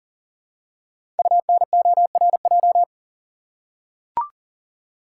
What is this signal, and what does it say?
Morse code, heard as a keyed tone, about 39 wpm, 700 Hz. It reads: UNORJ
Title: Code Groups - 5 Letters - 40wpm